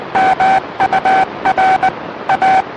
Morse code, heard as a keyed tone, about 20 wpm, 760 Hz, with noise. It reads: MUR A
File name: satellite / AO51 CW